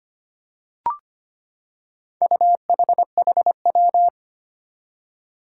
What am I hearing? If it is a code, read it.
Morse code, heard as a keyed tone, about 25 wpm, 700 Hz. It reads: UHHW
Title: Four Challenging Letter-Number Combinations 25wpm